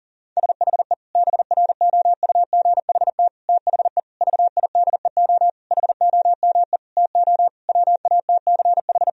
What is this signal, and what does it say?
Morse code, heard as a keyed tone, about 40 wpm, 700 Hz. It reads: SHE BROUGHT THE VIDEO HOME TO WATCH
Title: Sentences from Top 1000 Words - Review Part 4 - 40wpm